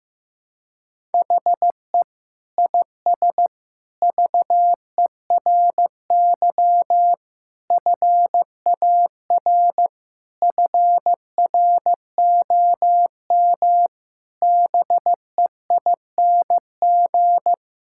Morse code, heard as a keyed tone, about 15 wpm, 700 Hz. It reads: HE IS VERY FAR FROM BEING
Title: Sentences from Top 1000 Words - Review Part 6 - 15wpm